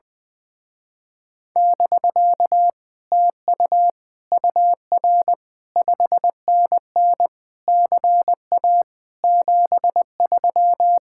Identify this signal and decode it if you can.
Morse code, heard as a keyed tone, about 20 wpm, 700 Hz. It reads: <BK> TU UR 5NN CA 73
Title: Speed-Racing Common QSO Phrases - 20wpm